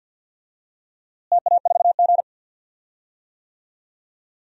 Morse code, heard as a keyed tone, about 49 wpm, 700 Hz. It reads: TA4G
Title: DXCC-100 Call Signs - 4 Characters - Repeated 2x - 50wpm